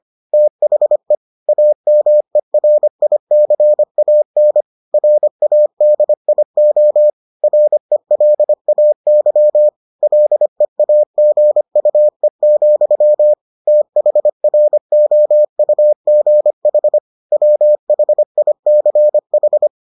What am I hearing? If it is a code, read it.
Morse code, heard as a keyed tone, about 25 wpm, 600 Hz. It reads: THE AMERICAN RADIO RELAY LEAGUE, THROUGH WHICH